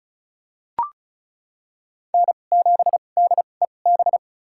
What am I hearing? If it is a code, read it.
Morse code, heard as a keyed tone, about 34 wpm, spaced as for 25 wpm, 700 Hz. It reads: N7DEB
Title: USA Call Signs - 5 Characters - 2x Character Spacing - 35wpm